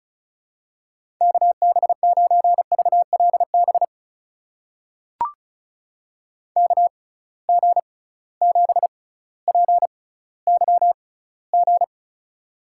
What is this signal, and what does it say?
Morse code, heard as a keyed tone, about 34 wpm, spaced as for 18 wpm, 700 Hz. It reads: KB9VLB KG7PYG
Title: USA Call Signs - 6 Characters - 6x Character Spacing - 35wpm